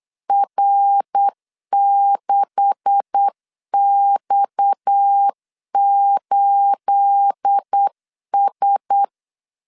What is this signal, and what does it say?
Morse code, heard as a keyed tone, about 8 wpm, 800 Hz. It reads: R6X8S
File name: CW / DET / 1 9 1500C S A 8wpm 5637